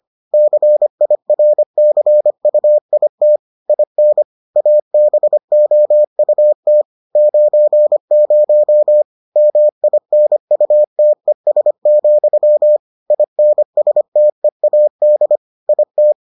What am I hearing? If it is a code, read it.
Morse code, heard as a keyed tone, about 25 wpm, 600 Hz. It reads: CIRCUIT IN ABOUT 90 MINUTES, INSTEAD IT